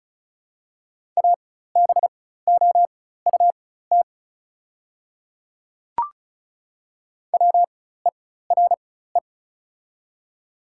Morse code, heard as a keyed tone, about 36 wpm, spaced as for 16 wpm, 700 Hz. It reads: ABOUT WERE
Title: 4x Character Spacing - Top 100 Words - 35wpm